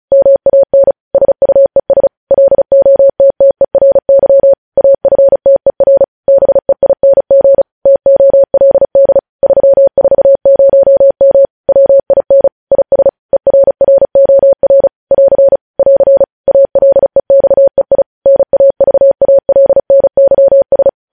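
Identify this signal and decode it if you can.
Morse code, heard as a keyed tone, about 36 wpm, 550 Hz. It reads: MAN SUES LOTTERY AFTER BEING TOLD 340M WIN IS ERROR <AR> <AR> ALEXEI NAVALNYS
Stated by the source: Morse code news headlines sent at 35 words per minute.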